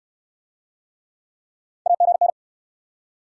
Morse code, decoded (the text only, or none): AGN